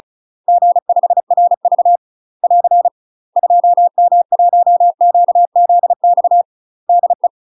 Morse code, heard as a keyed tone, about 36 wpm, 700 Hz. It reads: G5RV <AR> 2M1QZX DE
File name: UK MASTER QSOs TRAINING / 35WPM / UK MASTER QSOs 0033